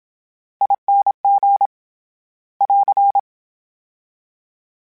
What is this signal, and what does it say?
Morse code, heard as a keyed tone, about 27 wpm, 800 Hz. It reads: ING <AR>
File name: Special INGs / Special INGs-mp3s / VerySlowToFaster Ing @25